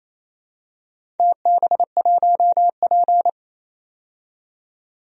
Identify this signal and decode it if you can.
Morse code, heard as a keyed tone, about 28 wpm, 700 Hz. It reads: TB1P
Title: DXCC-100 Call Signs - 4 Characters - 28wpm